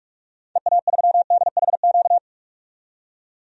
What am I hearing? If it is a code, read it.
Morse code, heard as a keyed tone, about 46 wpm, 700 Hz. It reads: EA3DHQ
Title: DXCC-100 Call Signs - 6 Characters - 45wpm